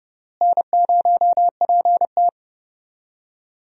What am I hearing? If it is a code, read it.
Morse code, heard as a keyed tone, about 31 wpm, 700 Hz. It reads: N0PT
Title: USA Call Signs - 4 Characters - Repeated 3x - 30wpm